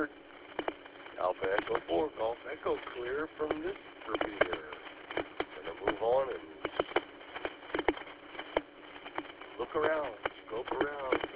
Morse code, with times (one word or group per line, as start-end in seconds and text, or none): none